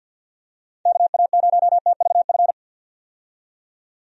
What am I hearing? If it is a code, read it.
Morse code, heard as a keyed tone, about 50 wpm, 700 Hz. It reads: KA0TVF
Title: USA Call Signs - 6 Characters - 50wpm